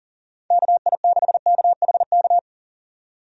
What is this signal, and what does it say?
Morse code, heard as a keyed tone, about 40 wpm, 700 Hz. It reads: KI6KHK
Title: USA Call Signs - 6 Characters - 40wpm